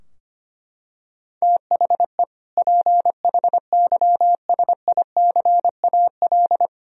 Morse code, heard as a keyed tone, about 25 wpm, 700 Hz. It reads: THE PHYSICAL
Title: Sentences from Top 1000 Words - Review Part 4 - 25wpm